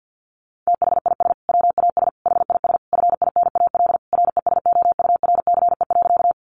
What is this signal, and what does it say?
Morse code, heard as a keyed tone, about 50 wpm, 700 Hz. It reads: THIS WAS HIS FINAL RESOURCE.
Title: Sentences from Top 1300 Words 50wpm